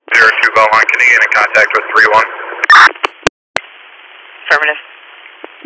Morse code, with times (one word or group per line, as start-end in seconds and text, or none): none